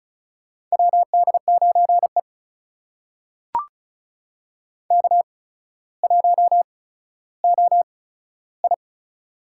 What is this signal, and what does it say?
Morse code, heard as a keyed tone, about 36 wpm, spaced as for 15 wpm, 700 Hz. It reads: WD9E K1OI
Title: USA Call Signs - 8x Character Spacing - 35wpm